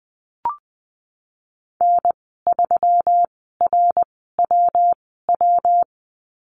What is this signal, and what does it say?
Morse code, heard as a keyed tone, about 20 wpm, spaced as for 14 wpm, 700 Hz. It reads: N3RWW
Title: 161 Lesson 26g - Letter g - Call Signs 20wpm